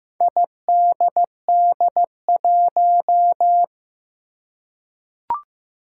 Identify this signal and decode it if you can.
Morse code, heard as a keyed tone, about 15 wpm, 700 Hz. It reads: IDD1
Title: Four Challenging Letter-Number Combinations 15wpm